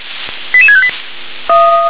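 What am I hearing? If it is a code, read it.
Five-tone selective call (ZVEI), heard as quick runs of five tones, tones of about 70 ms. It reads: 88658
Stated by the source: Five-tone ZVEI selective call received on an SDR.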